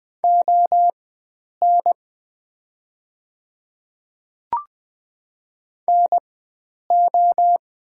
Morse code, heard as a keyed tone, about 20 wpm, spaced as for 9 wpm, 700 Hz. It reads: ON NO
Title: 4x Character Spacing - Top 100 Words - 20wpm